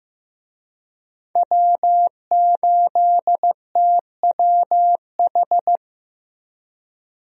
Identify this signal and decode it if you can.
Morse code, heard as a keyed tone, about 15 wpm, 700 Hz. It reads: W8TWH
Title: USA Call Signs - 5 Characters - 15wpm